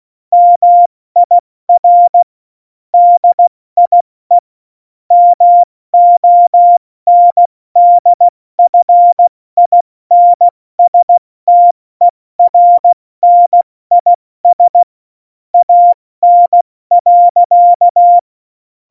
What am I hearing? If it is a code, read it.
Morse code, heard as a keyed tone, about 16 wpm, 700 Hz. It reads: MIR DIE MONDFINSTERNIS AN.
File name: Morsekurs / Witze / Witz 21. November 2017, 70 F80